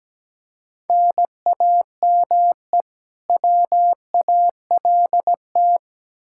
Morse code, heard as a keyed tone, about 17 wpm, 700 Hz. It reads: NAME WALT
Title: Practice QSO - Exchange 1 - Beginning - 17wpm